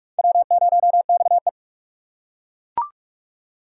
Morse code, heard as a keyed tone, about 41 wpm, 700 Hz. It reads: W0XE